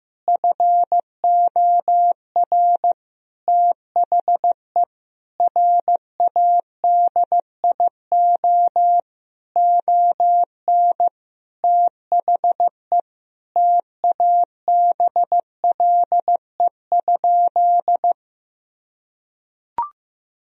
Morse code, heard as a keyed tone, about 15 wpm, 700 Hz. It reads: FOR THE RADIO ON THE TABLE?
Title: Sentences from Top 1000 Words - Review Part 3 - 15wpm